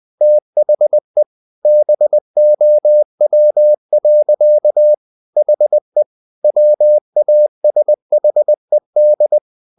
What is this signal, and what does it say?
Morse code, heard as a keyed tone, about 20 wpm, 600 Hz. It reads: THE BOW. HE WASHED
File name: the old man and the sea part 2 20wpm